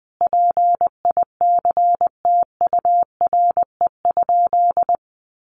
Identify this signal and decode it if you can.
Morse code, heard as a keyed tone, about 20 wpm, 700 Hz. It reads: PICTURE?
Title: A Sound Of Thunder 20wpm